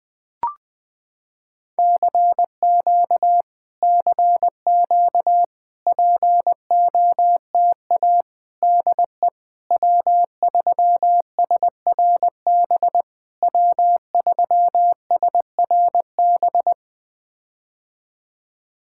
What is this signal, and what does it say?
Morse code, heard as a keyed tone, about 20 wpm, 700 Hz. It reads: CQ CQ POTA DE W3SRB W3SRB
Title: POTA QSO Practice - 20wpm